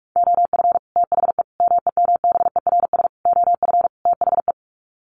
Morse code, heard as a keyed tone, about 44 wpm, 700 Hz. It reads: OF THE MEMBERS OF THE
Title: Sets of 5 Words Encore2 45wpm